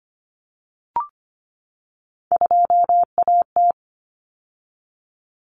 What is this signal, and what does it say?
Morse code, heard as a keyed tone, about 25 wpm, 700 Hz. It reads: 2AT
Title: Common Cut Numbers - 3-Digit 25wpm